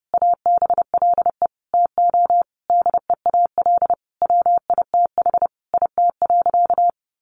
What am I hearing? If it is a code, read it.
Morse code, heard as a keyed tone, about 28 wpm, 700 Hz. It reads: ABLE TO DEAL WITH IT.
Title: Sentences from Top 800 Words 30wpm